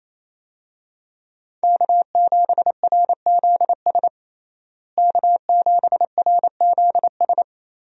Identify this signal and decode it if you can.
Morse code, heard as a keyed tone, about 28 wpm, 700 Hz. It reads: K7RZS K7RZS
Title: USA Call Signs - 5 Characters - Repeated 3x - 28wpm